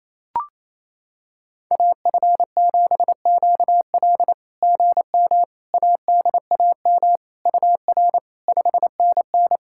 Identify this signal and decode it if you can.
Morse code, heard as a keyed tone, about 27 wpm, 660 Hz. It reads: AF7QL GM ADAM UR 5NN
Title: SOTA QSO Practice 28wpm